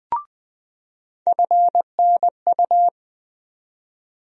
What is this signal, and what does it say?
Morse code, heard as a keyed tone, about 20 wpm, 700 Hz. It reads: FNU
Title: USA Call Sign Postfixes - 3 Letters 20wpm